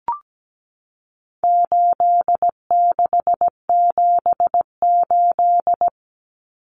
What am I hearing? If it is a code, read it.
Morse code, heard as a keyed tone, about 17 wpm, 700 Hz. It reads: 8678
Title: Numbers - 4-Digit 17wpm